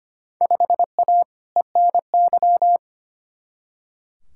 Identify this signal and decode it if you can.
Morse code, heard as a keyed tone, about 25 wpm, 700 Hz. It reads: 5A ENY
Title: Field Day Exchanges Encore - 25wpm